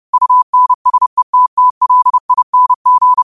Morse code, heard as a keyed tone, about 30 wpm, 1000 Hz. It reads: UNSETTLING